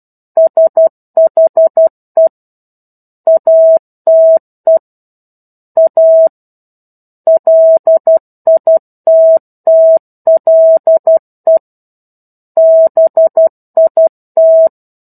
Morse code, heard as a keyed tone, about 12 wpm, 650 Hz. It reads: SHE ATE A LITTLE BIT
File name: 01-Chapter-01 12 WPM